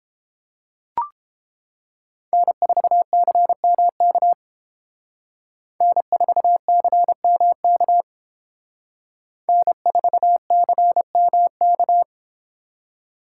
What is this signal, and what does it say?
Morse code, heard as a keyed tone, about 30 wpm, 700 Hz. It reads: N4CMK N4CMK N4CMK